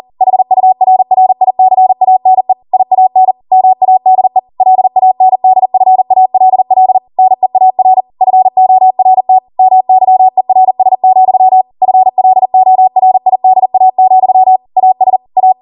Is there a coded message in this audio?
Yes: Morse code from 0.2 to 15.5 s, heard as a keyed tone, about 40 wpm, 750 Hz. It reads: HURRICANE IAN MADE LANDFALL NEAR FORT MYERS, FLORIDA, AS A